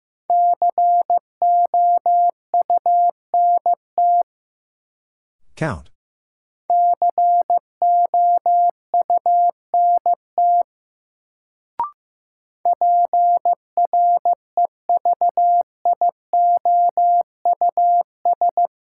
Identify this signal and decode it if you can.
Morse code, heard as a keyed tone, about 15 wpm, 700 Hz. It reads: COUNT COUNT PREVIOUS
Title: Sentences from Top 1500 Words - New Word Drill Alt - 15wpm